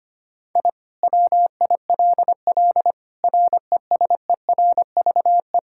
Morse code, heard as a keyed tone, about 25 wpm, 700 Hz. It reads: I WILL RESERVE
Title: Sentences from Top 1000 Words - Review Part 6 - 25wpm